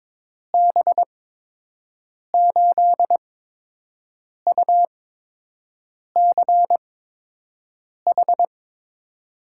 Morse code, heard as a keyed tone, about 22 wpm, spaced as for 6 wpm, 700 Hz. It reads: B8UCH
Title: USA Call Signs - 6 Characters - 8x Character Spacing - 22wpm